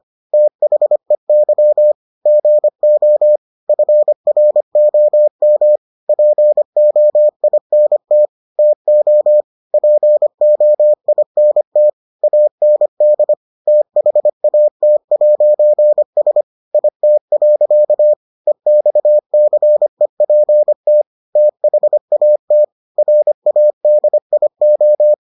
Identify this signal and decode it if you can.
Morse code, heard as a keyed tone, about 25 wpm, 600 Hz. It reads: THEY GO FROM POINT TO POINT AND THAT'S IT. EXCEPT THAT RADIO